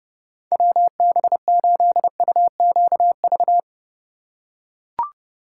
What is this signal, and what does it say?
Morse code, heard as a keyed tone, about 31 wpm, 700 Hz. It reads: WB8UQV